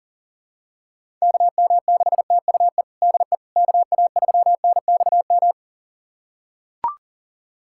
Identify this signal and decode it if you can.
Morse code, heard as a keyed tone, about 40 wpm, 700 Hz. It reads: KM6TUE DE KA3NXM
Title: USA Callsign DE Callsign 40wpm